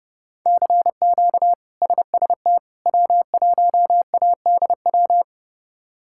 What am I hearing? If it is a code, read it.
Morse code, heard as a keyed tone, about 32 wpm, 700 Hz. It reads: CQ SST W1ADW
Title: K1USN SST QSO Practice - 30wpm